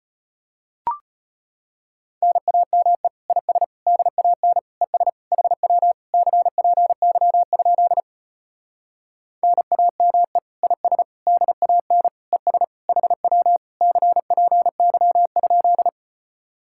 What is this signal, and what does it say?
Morse code, heard as a keyed tone, about 33 wpm, 700 Hz. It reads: NAME IS DAN ES HW CPY? NAME IS DAN ES HW CPY?